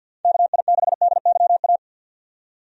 Morse code, heard as a keyed tone, about 52 wpm, 700 Hz. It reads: KI6DYA
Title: USA Call Signs - 6 Characters - 50wpm